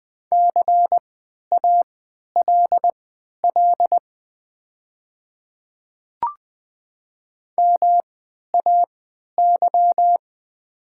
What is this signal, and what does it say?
Morse code, heard as a keyed tone, about 20 wpm, spaced as for 11 wpm, 700 Hz. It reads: CALL MAY
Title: 3x Character Spacing - Top 300 Words - 20wpm